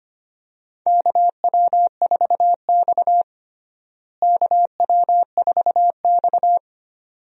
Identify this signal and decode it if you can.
Morse code, heard as a keyed tone, about 25 wpm, 700 Hz. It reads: KW4X KW4X
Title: USA Call Signs - 4 Characters - Repeated 2x - 25wpm